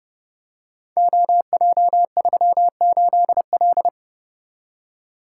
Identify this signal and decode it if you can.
Morse code, heard as a keyed tone, about 30 wpm, 700 Hz. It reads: OJ38L
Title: Code Groups - 5 Letter:Numbers - 30wpm